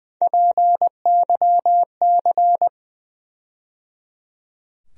Morse code, heard as a keyed tone, about 20 wpm, 700 Hz. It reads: PYC
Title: USA Call Sign Postfixes - 3 Letters 20wpm